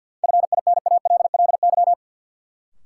Morse code, heard as a keyed tone, about 45 wpm, 700 Hz. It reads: FINALLY
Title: Sentences from Top 1000 Words - New Word Drill - 50wpm